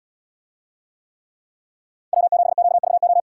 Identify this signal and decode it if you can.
Morse code, heard as a keyed tone, about 78 wpm, 700 Hz. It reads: W7QVO